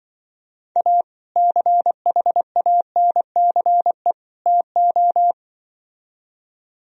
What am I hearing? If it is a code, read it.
Morse code, heard as a keyed tone, about 25 wpm, 700 Hz. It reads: A CHANCE TO
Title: Speed-Racing - Sets of 3 Words 20wpm